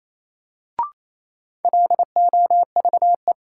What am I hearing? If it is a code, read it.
Morse code, heard as a keyed tone, about 28 wpm, 700 Hz. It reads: LOVE
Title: Sentences from Top 400 Words - New Word Drill - 28wpm